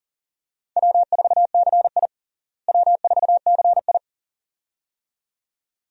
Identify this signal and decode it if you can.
Morse code, heard as a keyed tone, about 40 wpm, 700 Hz. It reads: W4CI W4CI
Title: USA Call Signs - 4 Characters - Repeated 2x - 40wpm